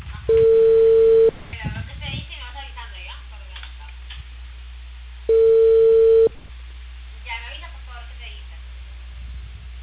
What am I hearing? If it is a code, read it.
Ringback tone (China, 450 Hz), with speech.